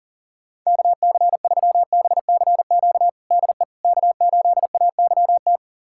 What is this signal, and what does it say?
Morse code, heard as a keyed tone, about 37 wpm, 700 Hz. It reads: KC3BCQ DE K8AYT
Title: USA Callsign DE Callsign 40wpm